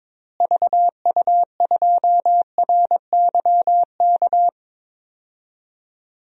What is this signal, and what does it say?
Morse code, heard as a keyed tone, about 22 wpm, 700 Hz. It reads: VU2RYK
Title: DXCC-100 Call Signs - 6 Characters - Repeated 3x - 22wpm